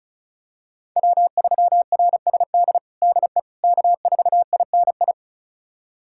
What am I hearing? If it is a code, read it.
Morse code, heard as a keyed tone, about 36 wpm, 700 Hz. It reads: W3RSD DE K4INI
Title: USA Callsign DE Callsign - 5 Characters - 35wpm